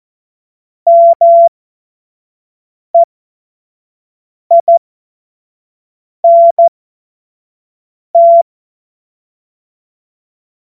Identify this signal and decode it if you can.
Morse code, heard as a keyed tone, about 14 wpm, spaced as for 5 wpm, 680 Hz. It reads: MEINT
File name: Morsekurs / Witze / Witz 06. Dezember 2022 25 F70